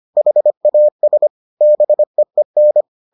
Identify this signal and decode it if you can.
Morse code, heard as a keyed tone, about 25 wpm, 600 Hz. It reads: HAS BEEN